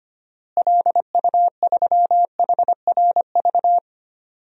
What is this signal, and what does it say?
Morse code, heard as a keyed tone, about 25 wpm, 700 Hz. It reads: LU3HRV
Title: DXCC-100 Call Signs - 6 Characters - Repeated 3x - 25wpm